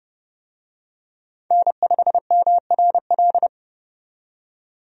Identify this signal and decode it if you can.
Morse code, heard as a keyed tone, about 31 wpm, 700 Hz. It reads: N5MRL
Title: USA Call Signs - 5 Characters - 30wpm